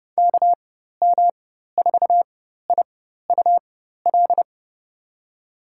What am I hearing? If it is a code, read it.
Morse code, heard as a keyed tone, about 30 wpm, spaced as for 14 wpm, 700 Hz. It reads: KM4IUL
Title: USA Call Signs - 6 Characters - 4x Character Spacing - 30wpm